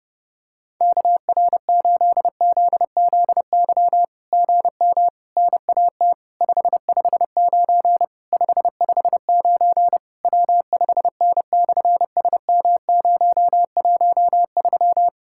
Morse code, heard as a keyed tone, about 30 wpm, 700 Hz. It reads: KR8ZZY GM NAT 559 559 W5N/SM013
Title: SOTA QSO Practice 30wpm